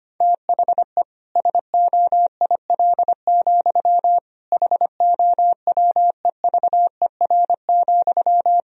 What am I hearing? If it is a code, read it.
Morse code, heard as a keyed tone, about 25 wpm, 700 Hz. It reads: THE SOIL, HOWEVER,
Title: Sentences from Top 1000 Words - Review Part 4 - 25wpm